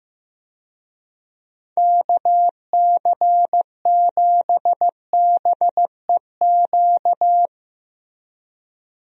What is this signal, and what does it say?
Morse code, heard as a keyed tone, about 15 wpm, 700 Hz. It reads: KC7BEQ
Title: USA Call Signs - 6 Characters - Repeated 2x - 15wpm